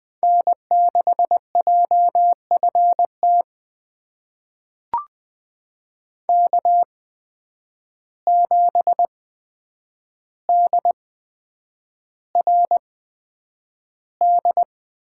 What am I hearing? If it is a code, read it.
Morse code, heard as a keyed tone, about 20 wpm, spaced as for 9 wpm, 700 Hz. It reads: N6JFT K7DRD
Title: USA Call Signs - 5 Characters - 8x Character Spacing - 20wpm